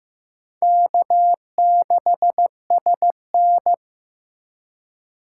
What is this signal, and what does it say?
Morse code, heard as a keyed tone, about 15 wpm, 700 Hz. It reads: K6SN
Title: USA Call Signs - 4 Characters - 4x Character Spacing - 15wpm